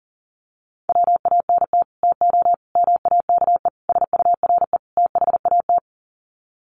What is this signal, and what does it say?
Morse code, heard as a keyed tone, about 39 wpm, 700 Hz. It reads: WANT TO MAKE SURE THAT
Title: Sets of 5 Words 40wpm